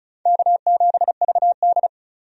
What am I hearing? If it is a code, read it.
Morse code, heard as a keyed tone, about 36 wpm, 700 Hz. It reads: K7VD
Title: USA Call Signs - 4 Characters - 35wpm